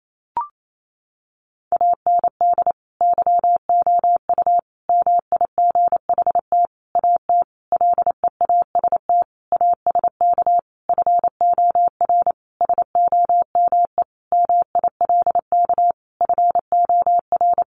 Morse code, heard as a keyed tone, about 28 wpm, 700 Hz. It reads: AND YOU MIGHT AT LEAST ASK FOR SOME MILK FOR
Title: Sentences from Top 1500 Words 28wpm